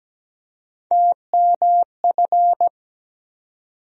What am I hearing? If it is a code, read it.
Morse code, heard as a keyed tone, about 17 wpm, 700 Hz. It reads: TMF
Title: USA Call Sign Postfixes - 3 Letters 17wpm